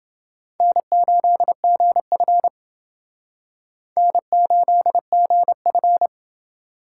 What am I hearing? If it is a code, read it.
Morse code, heard as a keyed tone, about 28 wpm, 700 Hz. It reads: N8GF N8GF